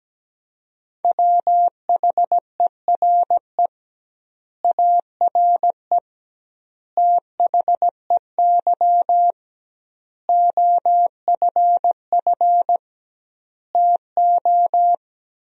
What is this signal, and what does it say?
Morse code, heard as a keyed tone, about 17 wpm, 700 Hz. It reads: WHERE ARE THEY OFF TO
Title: 2x Word-Spacing - Sentences from Top 200 Words - 17wpm